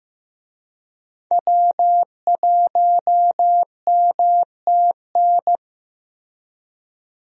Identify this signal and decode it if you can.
Morse code, heard as a keyed tone, about 15 wpm, 700 Hz. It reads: W1MTN
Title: USA Call Signs - 5 Characters - Repeated 3x - 15wpm